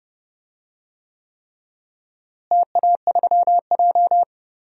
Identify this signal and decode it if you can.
Morse code, heard as a keyed tone, about 30 wpm, 700 Hz. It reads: TA3J
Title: DXCC-100 Call Signs - 4 Characters - Repeated 3x - 30wpm